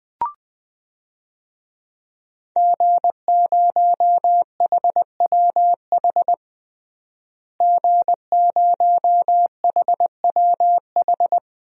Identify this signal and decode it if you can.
Morse code, heard as a keyed tone, about 20 wpm, 700 Hz. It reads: G0HWH G0HWH
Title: DXCC-100 Call Signs - 5 Characters - Repeated 3x - 20wpm